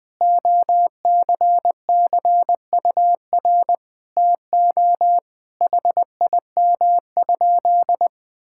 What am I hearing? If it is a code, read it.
Morse code, heard as a keyed tone, about 19 wpm, 700 Hz. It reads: OCCUR TO HIM?
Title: Sentences from Top 1000 Words - Review Part 6 - 20wpm